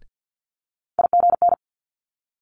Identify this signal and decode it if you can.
Morse code, heard as a keyed tone, about 50 wpm, 700 Hz. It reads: IGN
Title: Multi-letter Phonemes Quick Fire 50wpm